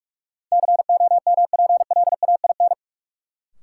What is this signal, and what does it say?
Morse code, heard as a keyed tone, about 45 wpm, 700 Hz. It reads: COMPLAIN
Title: Sentences from Top 1300 Words - New Word Drill Alt - 45wpm